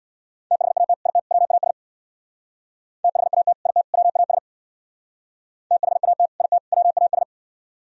Morse code, heard as a keyed tone, about 60 wpm, 700 Hz. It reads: THAT IT WAS THAT IT WAS THAT IT WAS
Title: Speed-Racing - Sets of 3 Words 50wpm